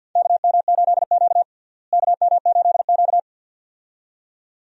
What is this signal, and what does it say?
Morse code, heard as a keyed tone, about 50 wpm, 700 Hz. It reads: KM8Q KM8Q
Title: USA Call Signs - 4 Characters - Repeated 2x - 50wpm